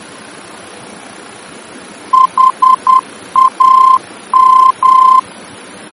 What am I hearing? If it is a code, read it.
Morse code, heard as a keyed tone, about 10 wpm, 1040 Hz. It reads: HAM
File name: germany / mx / HAM-D-VOR